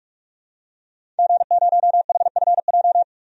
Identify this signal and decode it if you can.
Morse code, heard as a keyed tone, about 45 wpm, 700 Hz. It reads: G0HFJ
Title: DXCC-100 Call Signs - 5 Characters - 45wpm